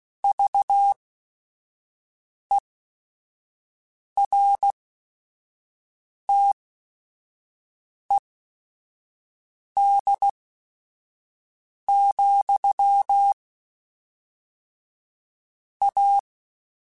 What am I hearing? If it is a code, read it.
Morse code, heard as a keyed tone, about 16 wpm, spaced as for 5 wpm, 800 Hz. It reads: VERTED, A
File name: Technical Subjects / Technical Subjects-mp3s / XC @5